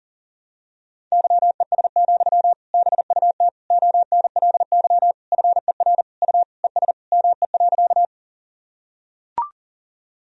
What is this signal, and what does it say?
Morse code, heard as a keyed tone, about 40 wpm, 700 Hz. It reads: YES, BUT ONLY FER U ES ME.
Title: Sentences from Top 500 Words and Abbreviations 40wpm